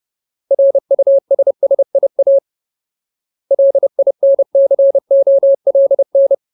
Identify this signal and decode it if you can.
Morse code, heard as a keyed tone, about 30 wpm, 550 Hz. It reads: RUSSIA LINCOLN
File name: Ultimate Word List / 30WPM Ultimate